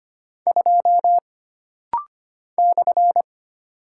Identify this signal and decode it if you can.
Morse code, heard as a keyed tone, about 25 wpm, 700 Hz. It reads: 2 /
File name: Single Character (Letter:Number:Punctuation:Prosign) - Rapid-Fire - 25wpm